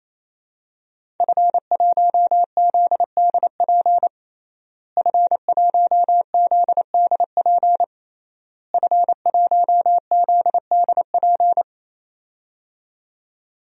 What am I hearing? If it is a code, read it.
Morse code, heard as a keyed tone, about 27 wpm, 700 Hz. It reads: F1ZDP F1ZDP F1ZDP
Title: DXCC-100 Call Signs - 5 Characters - Repeated 3x - 28wpm